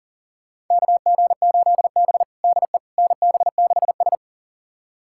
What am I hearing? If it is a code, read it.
Morse code, heard as a keyed tone, about 40 wpm, 700 Hz. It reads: KG8B DE NB6S
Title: USA Callsign DE Callsign - 4 Characters - 40wpm